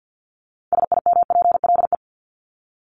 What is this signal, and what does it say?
Morse code, heard as a keyed tone, about 50 wpm, 700 Hz. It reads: SIMPLE
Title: LY Suffix - Top 100 Words with Base Words 50wpm